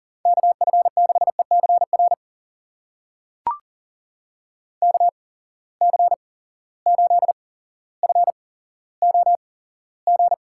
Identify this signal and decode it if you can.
Morse code, heard as a keyed tone, about 40 wpm, spaced as for 17 wpm, 700 Hz. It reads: KF6ECR KC8FOG
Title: USA Call Signs - 6 Characters - 8x Character Spacing - 40wpm